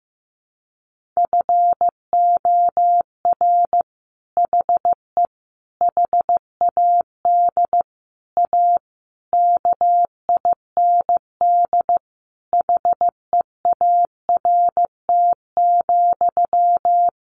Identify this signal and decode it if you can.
Morse code, heard as a keyed tone, about 15 wpm, 700 Hz. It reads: FOR HE HAD A KIND HEART,
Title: Sentences from Top 1400 Words 15wpm